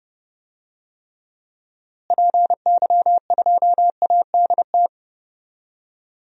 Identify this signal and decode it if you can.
Morse code, heard as a keyed tone, about 30 wpm, 700 Hz. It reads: PY2ADT
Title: DXCC-100 Call Signs - 6 Characters - Repeated 2x - 30wpm